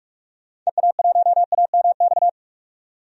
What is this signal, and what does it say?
Morse code, heard as a keyed tone, about 46 wpm, 700 Hz. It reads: EA1AMX
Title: DXCC-100 Call Signs - 6 Characters - 45wpm